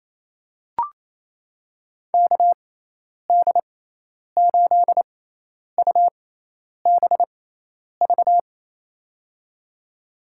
Morse code, heard as a keyed tone, about 27 wpm, spaced as for 10 wpm, 700 Hz. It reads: KD8UBV
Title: USA Call Signs - 6 Characters - 6x Character Spacing - 28wpm